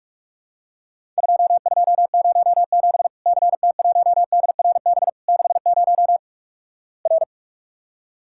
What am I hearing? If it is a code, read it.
Morse code, heard as a keyed tone, about 48 wpm, 700 Hz. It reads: 2207 CT1DRB 60 R
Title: Worked All Europe DX Contest - QTC Exchange - 45wpm